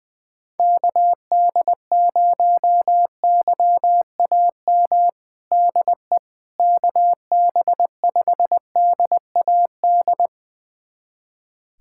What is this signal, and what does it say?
Morse code, heard as a keyed tone, about 20 wpm, 700 Hz. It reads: KD0YAM DE KB5DAD
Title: USA Callsign DE Callsign - 6 Characters - 20wpm